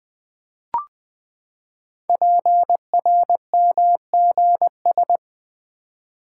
Morse code, heard as a keyed tone, about 20 wpm, 700 Hz. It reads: PRMGS
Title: Code Groups - 5 Letters - 20wpm